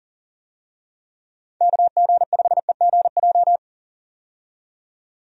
Morse code, heard as a keyed tone, about 40 wpm, 700 Hz. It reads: KG5EGJ
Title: USA Call Signs - 6 Characters - 40wpm